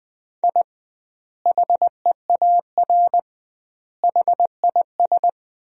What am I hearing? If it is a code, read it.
Morse code, heard as a keyed tone, about 20 wpm, 700 Hz. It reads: I HEAR HIS
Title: 2x Word-Spacing - Sentences from Top 500 Words - 20wpm